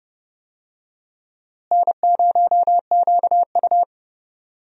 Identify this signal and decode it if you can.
Morse code, heard as a keyed tone, about 30 wpm, 700 Hz. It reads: N0QU
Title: USA Call Signs - 4 Characters - Repeated 2x - 30wpm